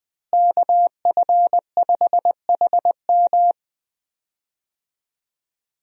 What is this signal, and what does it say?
Morse code, heard as a keyed tone, about 19 wpm, 700 Hz. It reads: KF5HM
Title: USA Call Signs - 5 Characters - Repeated 3x - 20wpm